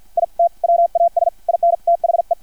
Morse code, heard as a keyed tone, about 50 wpm, spaced as for 44 wpm, 680 Hz. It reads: IT WAS IN THE
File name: archived / CRSnet